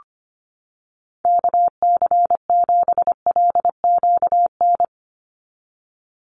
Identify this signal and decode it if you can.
Morse code, heard as a keyed tone, about 25 wpm, 700 Hz. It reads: KC7LQN
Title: USA Call Signs 25wpm